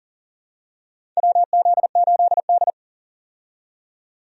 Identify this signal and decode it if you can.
Morse code, heard as a keyed tone, about 38 wpm, 690 Hz. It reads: WZ8D
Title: USA Call Signs - 4 Characters - 40wpm